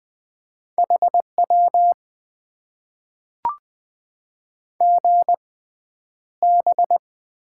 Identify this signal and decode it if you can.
Morse code, heard as a keyed tone, about 20 wpm, spaced as for 10 wpm, 700 Hz. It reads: HW GB
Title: Top 90 QSO Elements - 6x Character Spacing - 20wpm